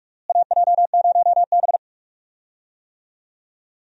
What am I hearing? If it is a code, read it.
Morse code, heard as a keyed tone, about 42 wpm, 700 Hz. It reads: AJ0B
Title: USA Call Signs - 4 Characters - Repeated 3x - 45wpm